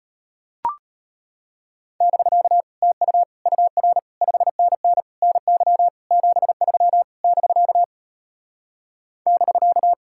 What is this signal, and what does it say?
Morse code, heard as a keyed tone, about 33 wpm, 700 Hz. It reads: <BK> TU UR 5NN NY 73 <BK> <BK>